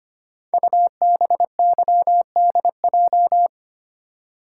Morse code, heard as a keyed tone, about 25 wpm, 700 Hz. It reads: UBYDJ
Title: Code Groups - 5 Letters - 25wpm